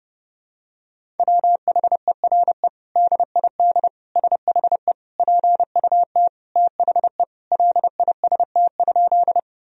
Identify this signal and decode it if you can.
Morse code, heard as a keyed tone, about 30 wpm, 700 Hz. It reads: WHERE DID SHE PUT THE LIST?
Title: Sentences from Top 1000 Words - Review Part 2 - 30wpm